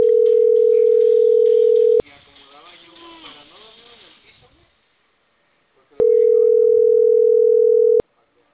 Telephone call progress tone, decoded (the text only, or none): ringback tone